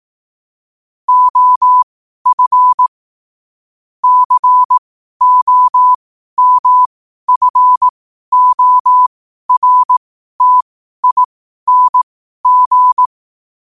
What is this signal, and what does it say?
Morse code, heard as a keyed tone, about 18 wpm, spaced as for 12 wpm, 1000 Hz. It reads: OF COMFORTING
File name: qotdfiles / 12wpm